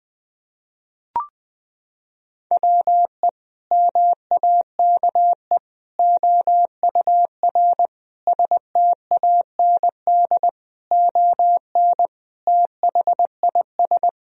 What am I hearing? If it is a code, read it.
Morse code, heard as a keyed tone, about 20 wpm, 700 Hz. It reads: WE MAKE OUR STAND ON THIS
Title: Sentences from Top 1000 Words - Review Part 2 - 20wpm